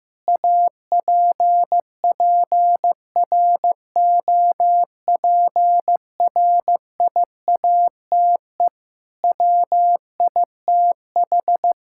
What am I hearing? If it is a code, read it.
Morse code, heard as a keyed tone, about 16 wpm, 700 Hz. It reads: APPROPRIATE WITH
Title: Sentences from Top 1000 Words - Review Part 5 - 15wpm